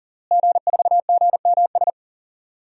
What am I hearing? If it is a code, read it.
Morse code, heard as a keyed tone, about 40 wpm, 700 Hz. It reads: G4GMS
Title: DXCC-100 Call Signs - 5 Characters - Repeated 3x - 40wpm